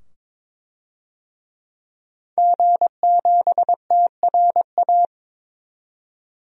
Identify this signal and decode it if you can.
Morse code, heard as a keyed tone, about 22 wpm, 700 Hz. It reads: G7TRA
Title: DXCC-100 Call Signs - 5 Characters - 22wpm